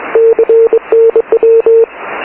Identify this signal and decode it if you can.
Morse code, heard as a keyed tone, about 23 wpm, 450 Hz. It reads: CNW